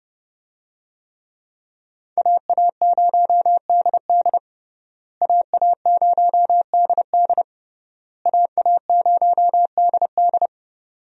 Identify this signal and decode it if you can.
Morse code, heard as a keyed tone, about 30 wpm, 700 Hz. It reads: AA0DD AA0DD AA0DD
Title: USA Call Signs - 5 Characters - Repeated 3x - 30wpm